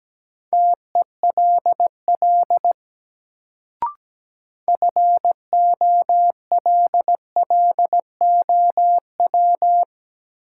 Morse code, heard as a keyed tone, about 17 wpm, 700 Hz. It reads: TELL FOLLOW
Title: Sentences from Top 200 Words - New Word Drill - 17wpm